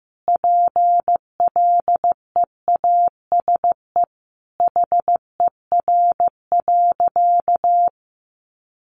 Morse code, heard as a keyed tone, about 15 wpm, 700 Hz. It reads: PLEASE HER.
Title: Sentences from Top 1500 Words 15wpm